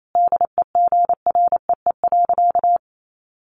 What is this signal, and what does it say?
Morse code, heard as a keyed tone, about 28 wpm, 700 Hz. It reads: DEGREE.
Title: Sentences from Top 1500 Words 28wpm